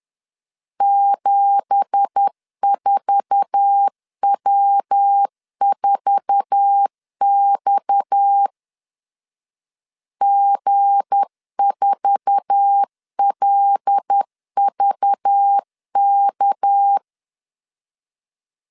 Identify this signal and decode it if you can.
Morse code, heard as a keyed tone, about 10 wpm, 800 Hz. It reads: 74W4X G4LVK
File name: CW / DET / 1 9 1500C S A 10wpm 4511